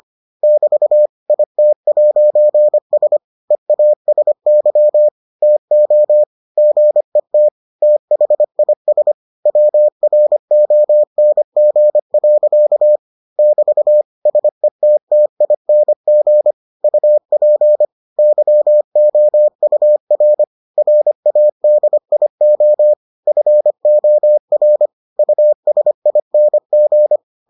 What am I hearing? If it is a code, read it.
Morse code, heard as a keyed tone, about 25 wpm, 600 Hz. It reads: <BT> IT'S EASY TO GET THIS WRONG. <BT> SETTING UP YOUR RADIO FOR USING